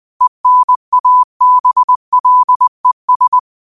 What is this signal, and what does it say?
Morse code, heard as a keyed tone, about 20 wpm, 1000 Hz. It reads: ENABLES